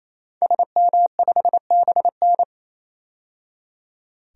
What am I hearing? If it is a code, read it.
Morse code, heard as a keyed tone, about 28 wpm, 700 Hz. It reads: SM5BN
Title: DXCC-100 Call Signs - 5 Characters - 28wpm